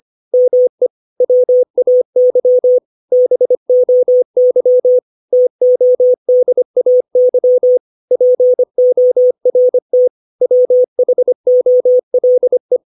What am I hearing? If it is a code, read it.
Morse code, heard as a keyed tone, about 25 wpm, 500 Hz. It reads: ME WAY BOY TODAY PORT WHOLE